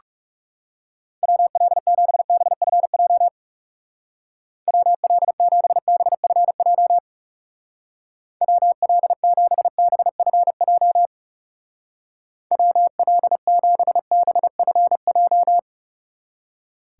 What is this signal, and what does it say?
Morse code, heard as a keyed tone, about 37 wpm, 700 Hz. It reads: WL7BFJ WL7BFJ WL7BFJ WL7BFJ
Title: Speed Racing - USA 2x3 Call Signs 30wpm